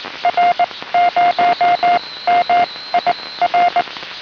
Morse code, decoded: R0MIR